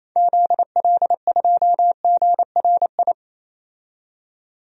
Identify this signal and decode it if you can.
Morse code, heard as a keyed tone, about 28 wpm, 700 Hz. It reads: ZL2GRI
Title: DXCC-100 Call Signs - 6 Characters - 28wpm